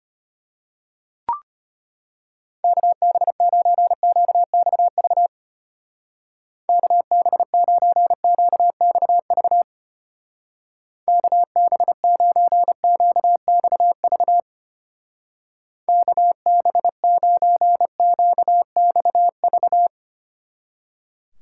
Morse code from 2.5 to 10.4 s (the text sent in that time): KB9QXV KB9QXV